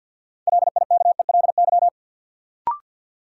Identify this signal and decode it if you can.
Morse code, heard as a keyed tone, about 52 wpm, 700 Hz. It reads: LIKELY
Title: Sets of 2 Words - New Word Drill - 50wpm